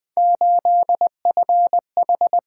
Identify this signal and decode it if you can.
Morse code, heard as a keyed tone, about 20 wpm, 700 Hz. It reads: 8FH
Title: Three Letter-Number Combinations - 20wpm